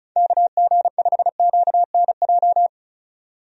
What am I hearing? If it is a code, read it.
Morse code, heard as a keyed tone, about 34 wpm, 700 Hz. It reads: KG5QNJ
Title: USA Call Signs - 6 Characters - 35wpm